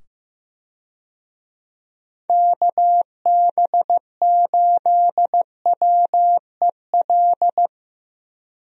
Morse code, heard as a keyed tone, about 15 wpm, 700 Hz. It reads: KB8WEL
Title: USA Call Signs - 6 Characters - 15wpm